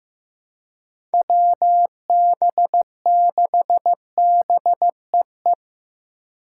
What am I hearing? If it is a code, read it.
Morse code, heard as a keyed tone, about 15 wpm, 700 Hz. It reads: WB6BEE
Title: K1USN SST QSO Practice - 15wpm